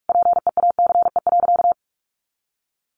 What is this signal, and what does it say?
Morse code, heard as a keyed tone, about 44 wpm, 700 Hz. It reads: PEACE.
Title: A Sound Of Thunder 45wpm